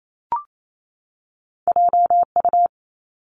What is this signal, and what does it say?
Morse code, heard as a keyed tone, about 28 wpm, 700 Hz. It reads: J U
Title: Two Challenging Letter-Number Combinations 28wpm